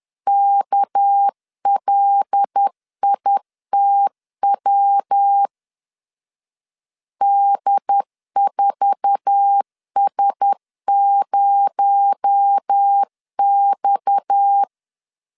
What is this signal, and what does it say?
Morse code, heard as a keyed tone, about 10 wpm, 800 Hz. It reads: KLITW D4S0X
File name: LEZIONI / 8 YJ45 / 8 YJ4 10wpm 2633